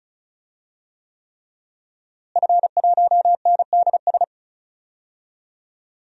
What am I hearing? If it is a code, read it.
Morse code, heard as a keyed tone, about 34 wpm, 700 Hz. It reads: F1NDS
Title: DXCC-100 Call Signs - 5 Characters - Repeated 3x - 35wpm